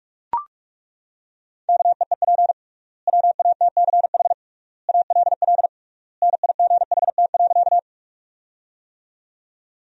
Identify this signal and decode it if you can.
Morse code, heard as a keyed tone, about 41 wpm, 700 Hz. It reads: KEEP WATCH ALL NIGHT.
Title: 3x Word-Spacing - Sentences from Top 300 Words - 45wpm